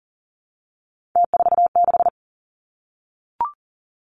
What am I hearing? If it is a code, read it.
Morse code, heard as a keyed tone, about 40 wpm, 700 Hz. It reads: T46
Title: Three Challenging Letter-Number Combinations 40wpm